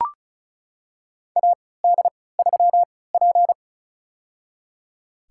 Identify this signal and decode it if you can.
Morse code, heard as a keyed tone, about 34 wpm, spaced as for 20 wpm, 700 Hz. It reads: AD3P
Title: USA Call Signs - 4 Characters - 3x Character Spacing - 35wpm